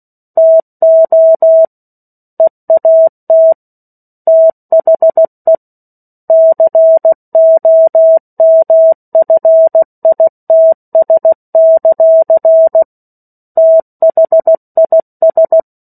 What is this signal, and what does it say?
Morse code, heard as a keyed tone, about 16 wpm, 650 Hz. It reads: TO EAT THE COMFITS; THIS
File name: 03-Chapter-03 16 WPM